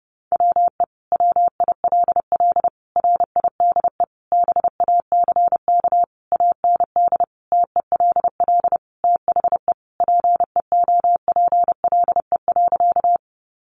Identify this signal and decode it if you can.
Morse code, heard as a keyed tone, about 30 wpm, 700 Hz. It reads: WE WILL RIDE BACK AND TELL THE PEOPLE.
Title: Sentences from Top 1200 Words 30wpm